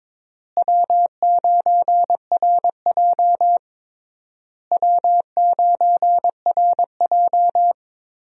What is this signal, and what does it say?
Morse code, heard as a keyed tone, about 22 wpm, 700 Hz. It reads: W9RJ W9RJ
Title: USA Call Signs - 4 Characters - Repeated 3x - 22wpm